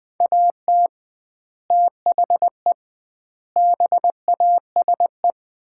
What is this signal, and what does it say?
Morse code, heard as a keyed tone, about 20 wpm, 700 Hz. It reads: AT THE BASE
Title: 2x Word-Spacing - Sets of 5 Words - 20wpm